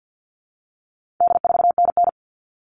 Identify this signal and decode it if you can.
Morse code, heard as a keyed tone, about 50 wpm, 700 Hz. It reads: D4NN
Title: All Cut Numbers - 4-Digit 50wpm